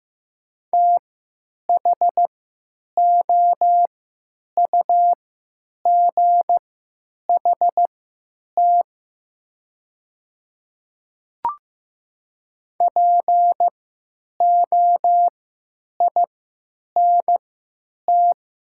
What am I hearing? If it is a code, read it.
Morse code, heard as a keyed tone, about 15 wpm, spaced as for 8 wpm, 700 Hz. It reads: THOUGHT POINT
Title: 3x Character Spacing - Top 300 Words - 15wpm